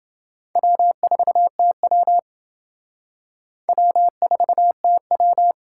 Morse code, heard as a keyed tone, about 28 wpm, 700 Hz. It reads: W4TW W4TW